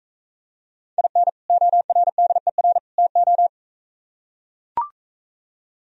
Morse code, heard as a keyed tone, about 44 wpm, 700 Hz. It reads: IN ORDER TO